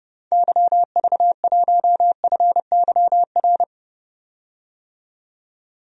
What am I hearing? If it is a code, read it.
Morse code, heard as a keyed tone, about 30 wpm, 700 Hz. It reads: YV1FYR
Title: DXCC-100 Call Signs - 6 Characters - 30wpm